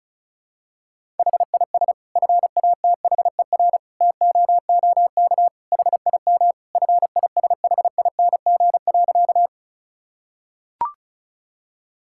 Morse code, heard as a keyed tone, about 36 wpm, 700 Hz. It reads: HIS FATHER TOOK HIM FISHING.
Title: Sentences from Top 1000 Words - Review Part 5 - 35wpm